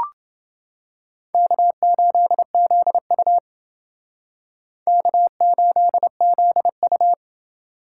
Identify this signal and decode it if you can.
Morse code, heard as a keyed tone, about 28 wpm, 700 Hz. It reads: K8ZU K8ZU